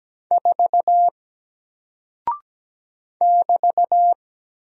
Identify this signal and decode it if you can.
Morse code, heard as a keyed tone, about 17 wpm, 700 Hz. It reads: 4 <BT>
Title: Single Character (Letter:Number:Punctuation:Prosign) Alt 17wpm